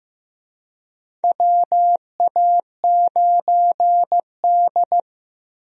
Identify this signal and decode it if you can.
Morse code, heard as a keyed tone, about 15 wpm, 700 Hz. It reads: WA9D
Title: USA Call Signs - 4 Characters - 15wpm